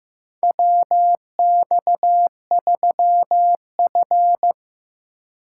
Morse code, heard as a keyed tone, about 15 wpm, 700 Hz. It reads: WX3F
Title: USA Call Signs - 2x Character Spacing - 15wpm